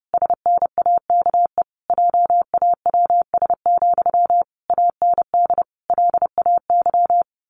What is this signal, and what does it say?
Morse code, heard as a keyed tone, about 30 wpm, 700 Hz. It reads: SNAKE JAWS, AND LAY
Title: A Sound Of Thunder 30wpm